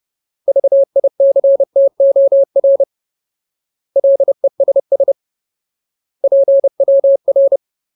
Morse code, heard as a keyed tone, about 30 wpm, 550 Hz. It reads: VICTOR LESS PWR
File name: Ultimate Word List / 30WPM Ultimate